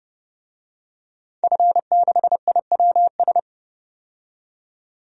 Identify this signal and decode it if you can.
Morse code, heard as a keyed tone, about 32 wpm, 700 Hz. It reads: F6IWS
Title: DXCC-100 Call Signs - 5 Characters - 30wpm